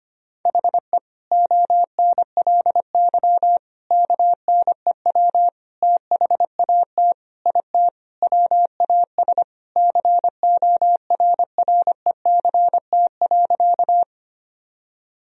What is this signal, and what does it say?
Morse code, heard as a keyed tone, about 25 wpm, 700 Hz. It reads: HE ONLY KNEW THAT IT WAS CORRECT.
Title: Sentences from Top 1000 Words - Review Part 2 - 25wpm